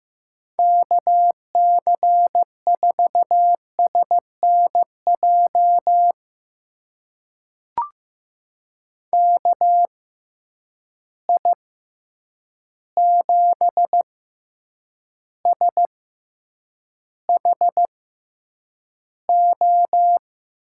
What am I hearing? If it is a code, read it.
Morse code, heard as a keyed tone, about 15 wpm, spaced as for 8 wpm, 700 Hz. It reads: KC4SNJ KI7SHO
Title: USA Call Signs - 6 Characters - 6x Character Spacing - 15wpm